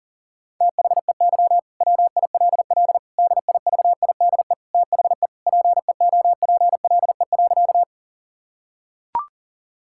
Morse code, heard as a keyed tone, about 40 wpm, 700 Hz. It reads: THEY WILL DIVIDE THE PEOPLE.
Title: Sentences from Top 1600 Words 40wpm